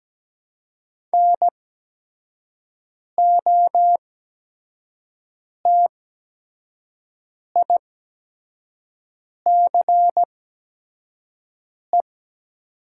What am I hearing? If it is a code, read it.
Morse code, heard as a keyed tone, about 17 wpm, spaced as for 5 wpm, 700 Hz. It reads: NOTICE
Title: 8x Character Spacing - Top 500 Words - 17wpm